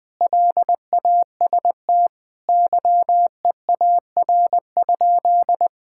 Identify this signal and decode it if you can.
Morse code, heard as a keyed tone, about 21 wpm, 700 Hz. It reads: LAST YEAR?
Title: Sentences from Top 1000 Words - Review Part 5 - 20wpm